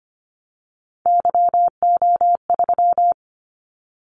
Morse code, heard as a keyed tone, about 25 wpm, 700 Hz. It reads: YO3
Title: Callsign Prefixes for Top 25 DXCC - 25wpm